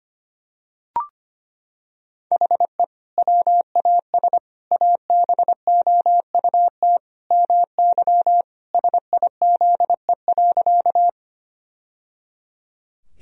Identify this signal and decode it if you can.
Morse code, heard as a keyed tone, about 25 wpm, 700 Hz. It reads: HE WAS ABOUT MY SIZE.
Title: Sentences from Top 1000 Words - Review Part 2 - 25wpm